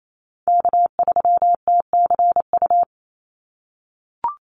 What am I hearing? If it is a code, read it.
Morse code, heard as a keyed tone, about 28 wpm, 700 Hz. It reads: K3TCU
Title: NA SOTA Chaser Callsigns 28wpm